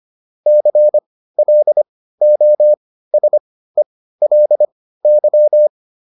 Morse code, heard as a keyed tone, about 25 wpm, spaced as for 15 wpm, 600 Hz. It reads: CLOSELY